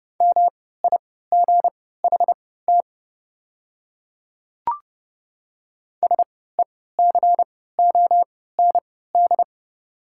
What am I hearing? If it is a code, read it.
Morse code, heard as a keyed tone, about 30 wpm, spaced as for 17 wpm, 690 Hz. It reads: MIGHT SECOND
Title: 3x Character Spacing - Top 300 Words - 30wpm